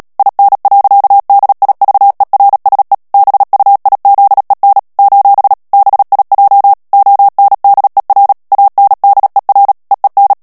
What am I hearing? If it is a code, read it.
Morse code, heard as a keyed tone, about 37 wpm, 800 Hz. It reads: IN. DIVERSE BUIZEN : BIJ ONDER ANDER EEN
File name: morse mp3 / HOLLANDAIS texte clair 2 vitesse 35 mots minute